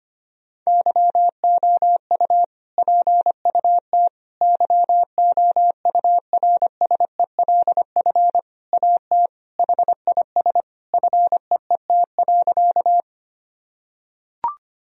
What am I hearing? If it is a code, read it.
Morse code, heard as a keyed tone, about 25 wpm, 700 Hz. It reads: YOU PUT YOURSELF AT HIS FEET.
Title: Sentences from Top 1000 Words - Review Part 1 - 25wpm